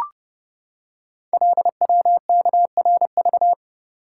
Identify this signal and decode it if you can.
Morse code, heard as a keyed tone, about 29 wpm, 700 Hz. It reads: LWKRV
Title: Code Groups - 5 Letters - 30wpm